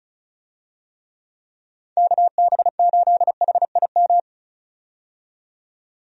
Morse code, heard as a keyed tone, about 36 wpm, 700 Hz. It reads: KB8HIM
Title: USA Call Signs - 6 Characters - 35wpm